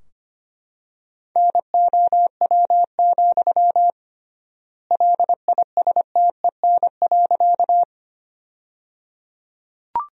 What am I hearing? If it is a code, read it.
Morse code, heard as a keyed tone, about 25 wpm, 700 Hz. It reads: NOW, LISTEN.
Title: 3x Word-Spacing - Sentences from Top 400 Words - 25wpm